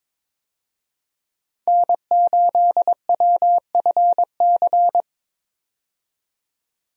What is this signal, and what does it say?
Morse code, heard as a keyed tone, about 22 wpm, 700 Hz. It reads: N8WFC
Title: USA Call Signs - 5 Characters - 22wpm